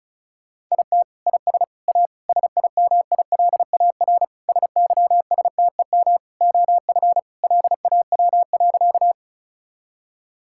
Morse code, heard as a keyed tone, about 36 wpm, 700 Hz. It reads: IT IS A SIMILAR SYSTEM OF LAW.
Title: Sentences from Top 1000 Words - Review Part 5 - 35wpm